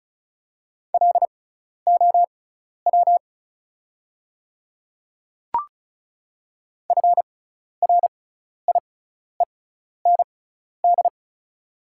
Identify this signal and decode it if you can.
Morse code, heard as a keyed tone, about 34 wpm, spaced as for 12 wpm, 690 Hz. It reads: LOW FRIEND
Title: 6x Character Spacing - Top 500 Words - 35wpm